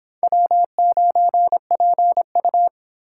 Morse code, heard as a keyed tone, about 23 wpm, 700 Hz. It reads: W9PU